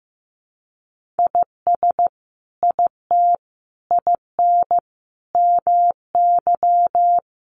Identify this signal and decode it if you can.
Morse code, heard as a keyed tone, about 15 wpm, 700 Hz. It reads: IS IT IN MY
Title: Sentences from Top 1200 Words 15wpm